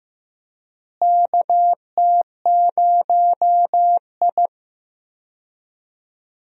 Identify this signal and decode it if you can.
Morse code, heard as a keyed tone, about 15 wpm, 700 Hz. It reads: KT0I
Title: USA Call Signs - 4 Characters - Repeated 2x - 15wpm